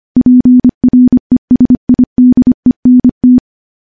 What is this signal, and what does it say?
Morse code, heard as a keyed tone, about 25 wpm, 250 Hz. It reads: PRESIDENT
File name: morse-training-25WPM